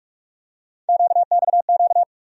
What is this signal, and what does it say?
Morse code, heard as a keyed tone, about 42 wpm, 700 Hz. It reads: QXQ
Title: USA Call Sign Postfixes - 3 Letters 45wpm